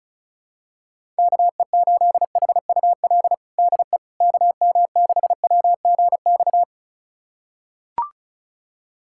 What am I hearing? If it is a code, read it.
Morse code, heard as a keyed tone, about 36 wpm, 700 Hz. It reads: KE8HUL DE KM6WGX
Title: USA Callsign DE Callsign 35wpm